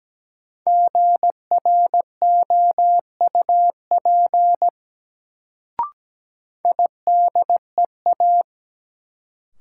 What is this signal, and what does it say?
Morse code, heard as a keyed tone, about 17 wpm, 700 Hz. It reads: GROUP IDEA
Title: Sets of 3 Words - New Word Drill - 17wpm